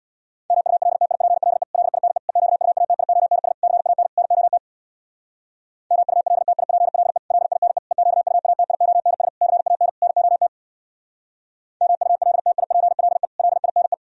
Morse code, heard as a keyed tone, about 65 wpm, 700 Hz. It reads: MULTIPLE LINE EQUATIONS CAN NOT MULTIPLE LINE EQUATIONS CAN NOT MULTIPLE LINE